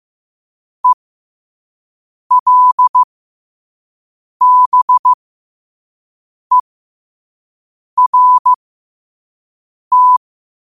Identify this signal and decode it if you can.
Morse code, heard as a keyed tone, about 15 wpm, spaced as for 5 wpm, 1000 Hz. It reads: ELBERT
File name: qotdfiles / 5wpm